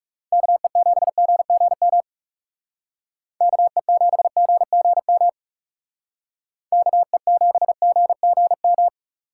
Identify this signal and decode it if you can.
Morse code, heard as a keyed tone, about 39 wpm, 700 Hz. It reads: KE7GGM KE7GGM KE7GGM